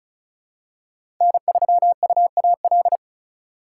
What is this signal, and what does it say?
Morse code, heard as a keyed tone, about 36 wpm, 700 Hz. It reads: N3UAL
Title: USA Call Signs - 5 Characters - 2x Character Spacing - 35wpm